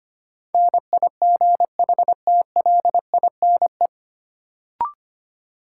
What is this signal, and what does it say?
Morse code, heard as a keyed tone, about 25 wpm, 700 Hz. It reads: NIGHTLINE
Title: Sets of 5 Words - New Word Drill - 25wpm